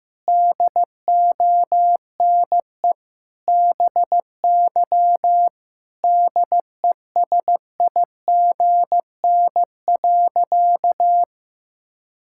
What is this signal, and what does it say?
Morse code, heard as a keyed tone, about 16 wpm, 700 Hz. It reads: DONE BY DESIGN.
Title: Sentences from Top 1000 Words - Review Part 5 - 15wpm